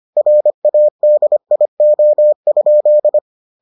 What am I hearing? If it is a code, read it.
Morse code, heard as a keyed tone, about 25 wpm, 600 Hz. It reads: RADIO?